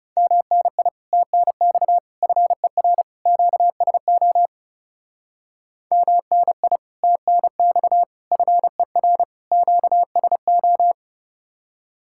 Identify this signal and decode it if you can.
Morse code, heard as a keyed tone, about 37 wpm, 700 Hz. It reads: MNI TNX FER QSO MNI TNX FER QSO
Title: Speed-Racing Common QSO Phrases - 30wpm